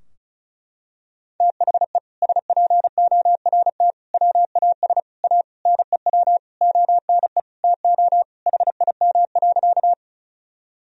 Morse code, heard as a keyed tone, about 36 wpm, 700 Hz. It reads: THE SPORT WAS A NEW ONE TO HIM.
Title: Sentences from Top 1000 Words - Review Part 4 - 35wpm